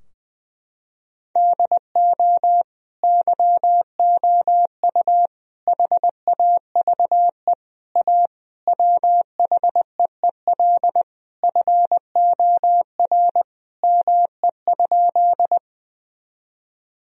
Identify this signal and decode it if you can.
Morse code, heard as a keyed tone, about 20 wpm, 700 Hz. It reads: DO YOU HAVE A WHEEL FOR ME?
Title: Sentences from Top 1000 Words - Review Part 3 - 20wpm